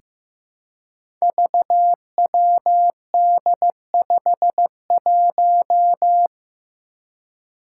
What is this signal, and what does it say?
Morse code, heard as a keyed tone, about 15 wpm, 700 Hz. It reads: VWD51
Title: Five Challenging Letter-Number Combinations 15wpm